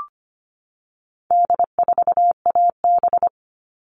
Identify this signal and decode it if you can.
Morse code, heard as a keyed tone, about 25 wpm, 700 Hz. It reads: D4AB
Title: All Cut Numbers - 4-Digit 25wpm